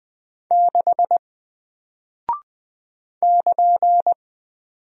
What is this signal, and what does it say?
Morse code, heard as a keyed tone, about 20 wpm, 700 Hz. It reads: 6 <KN>
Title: Single Character (Letter:Number:Punctuation:Prosign) Alt 20wpm